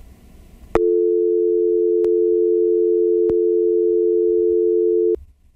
Dial tone (North America), 350 + 440 Hz.